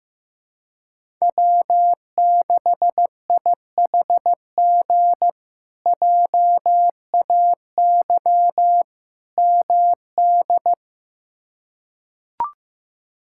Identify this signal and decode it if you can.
Morse code, heard as a keyed tone, about 16 wpm, 700 Hz. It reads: W6IHG JAY MD
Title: K1USN SST QSO Practice - 15wpm